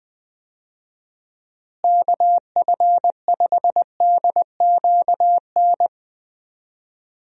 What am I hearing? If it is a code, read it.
Morse code, heard as a keyed tone, about 20 wpm, 700 Hz. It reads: KF5DQN
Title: USA Call Signs - 6 Characters - Repeated 2x - 20wpm